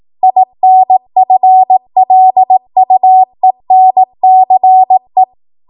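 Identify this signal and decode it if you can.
Morse code, heard as a keyed tone, about 18 wpm, 750 Hz. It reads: INFLUENCE
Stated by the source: Code practice at 18 WPM, a transition run bridging 15 to 20 WPM.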